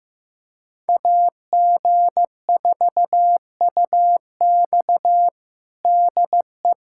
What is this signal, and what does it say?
Morse code, heard as a keyed tone, about 15 wpm, 700 Hz. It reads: AG4UX DE
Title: USA Callsign DE Callsign - 5 Characters - 15wpm